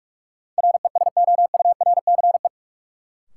Morse code, heard as a keyed tone, about 46 wpm, 700 Hz. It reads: RESOURCE
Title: Sentences from Top 1300 Words - New Word Drill Alt - 45wpm